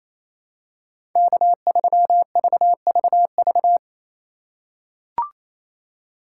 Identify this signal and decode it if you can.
Morse code, heard as a keyed tone, about 28 wpm, 700 Hz. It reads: K3VVV
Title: USA Call Signs - 5 Characters - 4x Character Spacing - 28wpm